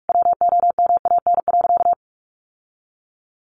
Morse code, heard as a keyed tone, about 44 wpm, 700 Hz. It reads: WOMAN.
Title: Sentences from Top 1100 Words 45wpm